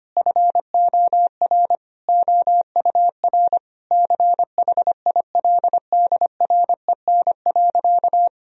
Morse code, heard as a keyed tone, about 25 wpm, 700 Hz. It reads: FOR OUR CHILDREN.
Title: Sentences from Top 1000 Words - Review Part 4 - 25wpm